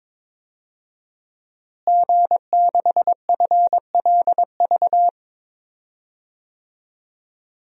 Morse code, heard as a keyed tone, about 22 wpm, 700 Hz. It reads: G6FLV
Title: DXCC-100 Call Signs - 5 Characters - Repeated 3x - 22wpm